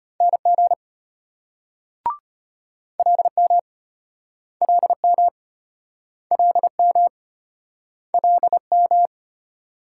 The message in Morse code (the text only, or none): NG LM LM LM LM